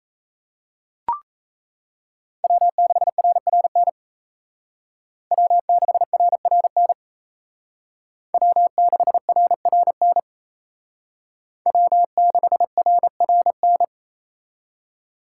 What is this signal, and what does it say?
Morse code, heard as a keyed tone, about 36 wpm, 700 Hz. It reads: W6RRN W6RRN W6RRN W6RRN